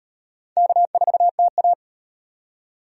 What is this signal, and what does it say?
Morse code, heard as a keyed tone, about 38 wpm, 700 Hz. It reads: K4TA